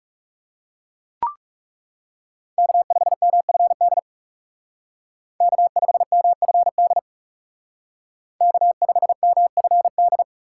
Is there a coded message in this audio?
Yes: Morse code from 2.6 to 10.2 s, heard as a keyed tone, about 38 wpm, 700 Hz. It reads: K5MFD K5MFD K5MFD